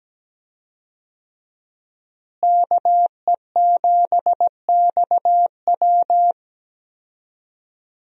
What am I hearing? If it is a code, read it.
Morse code, heard as a keyed tone, about 17 wpm, 700 Hz. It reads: KE7XW
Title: USA Call Signs - 5 Characters - Repeated 2x - 17wpm